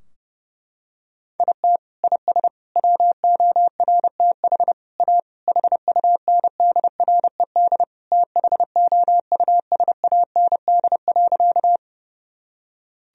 Morse code, heard as a keyed tone, about 30 wpm, 700 Hz. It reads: IT IS WORTH A HUNDRED THOUSAND.
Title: Sentences from Top 1000 Words - Review Part 3 - 30wpm